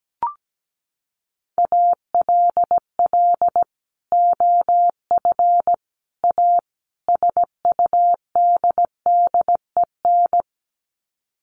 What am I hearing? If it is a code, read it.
Morse code, heard as a keyed tone, about 17 wpm, 700 Hz. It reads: ALL OF A SUDDEN
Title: Sets of 4 Words 17wpm